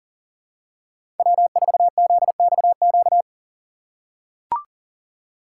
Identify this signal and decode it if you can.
Morse code, heard as a keyed tone, about 37 wpm, 700 Hz. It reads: W4ZXQ
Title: USA Call Signs - 5 Characters - 6x Character Spacing - 40wpm